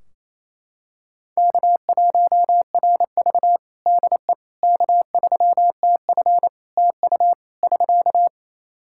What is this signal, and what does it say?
Morse code, heard as a keyed tone, about 28 wpm, 700 Hz. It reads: K1RV DE K3TF TU <SK>
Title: Practice QSO - Exchange 4 - Ending - 28wpm